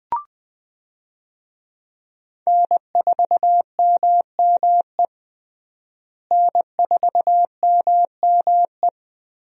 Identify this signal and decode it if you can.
Morse code, heard as a keyed tone, about 20 wpm, 700 Hz. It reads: N4MME N4MME
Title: USA Call Signs - 5 Characters - Repeated 3x - 20wpm